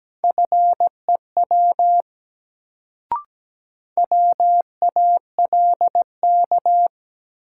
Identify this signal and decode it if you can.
Morse code, heard as a keyed tone, about 17 wpm, 700 Hz. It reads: FEW WALK
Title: Sentences from Top 300 Words - New Word Drill - 17wpm